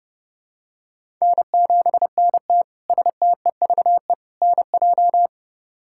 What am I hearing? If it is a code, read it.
Morse code, heard as a keyed tone, about 30 wpm, 700 Hz. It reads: N7NT STEVE NJ
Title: K1USN SST QSO Practice - 30wpm